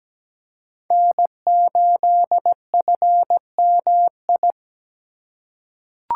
Morse code, heard as a keyed tone, about 17 wpm, 700 Hz. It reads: N8FMI
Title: USA Call Signs - 5 Characters - 2x Character Spacing - 17wpm